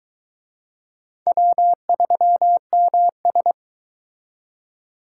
Morse code, heard as a keyed tone, about 22 wpm, 700 Hz. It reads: W3MS